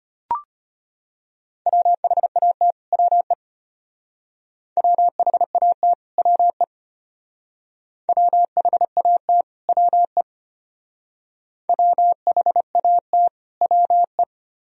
Morse code, heard as a keyed tone, about 33 wpm, 700 Hz. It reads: WHAT WE WHAT WE WHAT WE WHAT WE